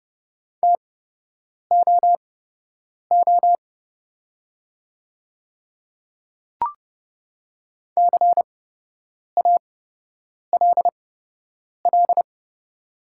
Morse code, heard as a keyed tone, about 30 wpm, spaced as for 8 wpm, 700 Hz. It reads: TOO CALL
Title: 8x Character Spacing - Top 200 Words - 30wpm